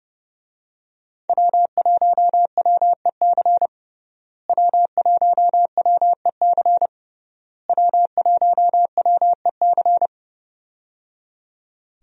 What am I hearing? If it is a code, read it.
Morse code, heard as a keyed tone, about 30 wpm, 700 Hz. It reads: W1WEC W1WEC W1WEC
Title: USA Call Signs - 5 Characters - Repeated 3x - 30wpm